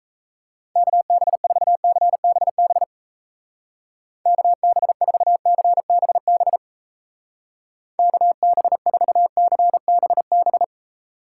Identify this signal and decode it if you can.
Morse code, heard as a keyed tone, about 36 wpm, 700 Hz. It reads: KB4CBB KB4CBB KB4CBB